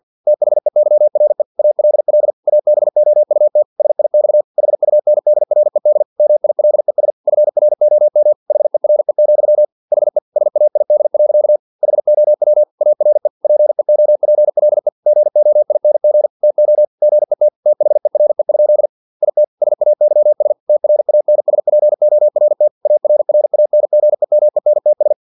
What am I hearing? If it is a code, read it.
Morse code, heard as a keyed tone, about 50 wpm, 600 Hz. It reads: THEYRE ALL ABOUT SIX HUNDRED MILES FROM HERE, HE SAID. HOW ARE PEOPLE GOING TO GET THERE? IT SAYS TRANSPORT ARRANGEMENTS
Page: Never 49wpm